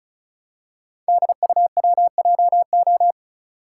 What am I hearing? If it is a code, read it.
Morse code, heard as a keyed tone, about 36 wpm, 690 Hz. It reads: DUWJO
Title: Code Groups - 5 Letter:Numbers - 35wpm